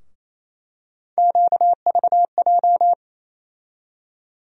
Morse code, heard as a keyed tone, about 28 wpm, 700 Hz. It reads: QVJ
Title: USA Call Sign Postfixes - 3 Letters 28wpm